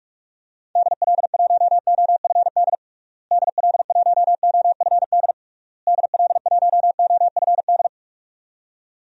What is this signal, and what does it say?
Morse code, heard as a keyed tone, about 46 wpm, 700 Hz. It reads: DL1OFD DL1OFD DL1OFD
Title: DXCC-100 Call Signs - 6 Characters - Repeated 3x - 45wpm